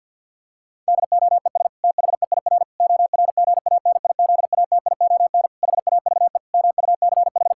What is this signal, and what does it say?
Morse code, heard as a keyed tone, about 48 wpm, 700 Hz. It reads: DOES THEIR ORGANIZATION HAVE MUCH
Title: Sentences from Top 1000 Words - Review Part 5 - 50wpm